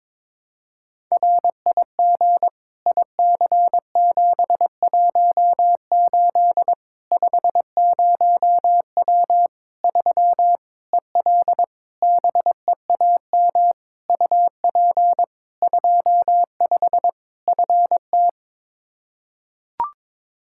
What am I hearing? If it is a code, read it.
Morse code, heard as a keyed tone, about 22 wpm, 700 Hz. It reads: RIG IC718 50W 3 EL BEAM UP 25 FT
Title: Practice QSO - Exchange 2 - Info - 22wpm